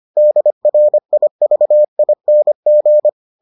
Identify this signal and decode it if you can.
Morse code, heard as a keyed tone, about 25 wpm, 600 Hz. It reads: DRIVING